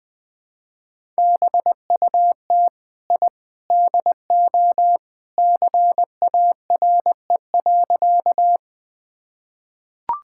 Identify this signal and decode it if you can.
Morse code, heard as a keyed tone, about 21 wpm, 700 Hz. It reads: BUT I DO CARE.
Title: Sentences with 3-Word Phrases - 20wpm